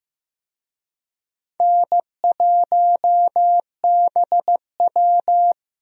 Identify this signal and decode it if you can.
Morse code, heard as a keyed tone, about 16 wpm, 700 Hz. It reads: N1BW
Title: USA Call Signs - 4 Characters - 15wpm